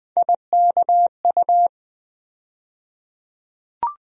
Morse code, heard as a keyed tone, about 20 wpm, 700 Hz. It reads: IKU
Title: USA Call Sign Postfixes - 3 Letters 20wpm